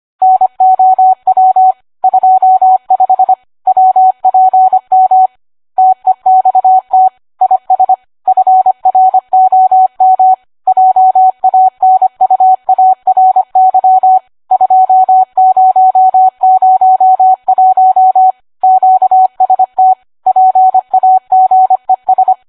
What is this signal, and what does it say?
Morse code, heard as a keyed tone, about 25 wpm, 750 Hz, with radio noise. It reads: NOW 25 WPM TEXT IS FROM JANUARY 2001 QST PAGES